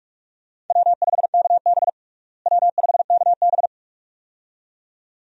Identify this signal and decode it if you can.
Morse code, heard as a keyed tone, about 47 wpm, 700 Hz. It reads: W5KB W5KB
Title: USA Call Signs - 4 Characters - Repeated 2x - 45wpm